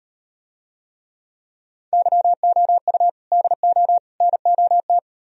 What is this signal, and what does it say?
Morse code, heard as a keyed tone, about 38 wpm, 700 Hz. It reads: YOU DO NOT